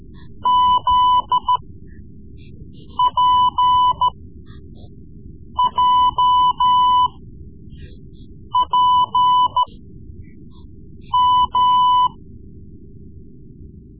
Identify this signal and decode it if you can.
Morse code, heard as a keyed tone, about 14 wpm, spaced as for 5 wpm, 1000 Hz. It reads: ZPJPM
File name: Morse Lessons / Morse Lessons / Tape-3B